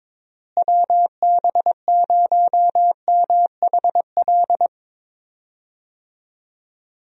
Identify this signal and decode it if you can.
Morse code, heard as a keyed tone, about 22 wpm, 700 Hz. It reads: WB0MHL
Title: USA Call Signs - 6 Characters - Repeated 2x - 22wpm